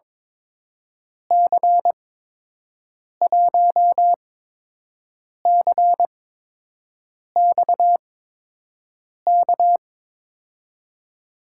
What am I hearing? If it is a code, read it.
Morse code, heard as a keyed tone, about 22 wpm, spaced as for 6 wpm, 700 Hz. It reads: C1CXK
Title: USA Call Signs - 6 Characters - 8x Character Spacing - 22wpm